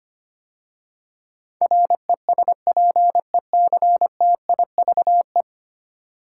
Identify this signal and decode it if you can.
Morse code, heard as a keyed tone, about 25 wpm, 700 Hz. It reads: RESPECTIVE
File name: 10 Letter Words - First 3 Letters Given - 25wpm